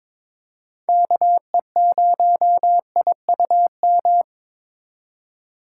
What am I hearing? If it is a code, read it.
Morse code, heard as a keyed tone, about 21 wpm, 700 Hz. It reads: KE0IUM
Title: USA Call Signs - 3x Character Spacing - 22wpm